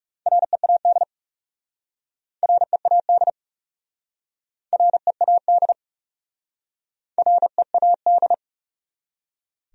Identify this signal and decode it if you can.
Morse code, heard as a keyed tone, about 36 wpm, 700 Hz. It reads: READ READ READ READ
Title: Speed-Racing Top 200 Words 30wpm